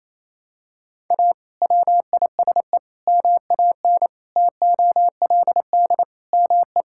Morse code, heard as a keyed tone, about 28 wpm, 700 Hz. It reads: A WISE MAN TOLD ME
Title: Sentences from Top 1600 Words 28wpm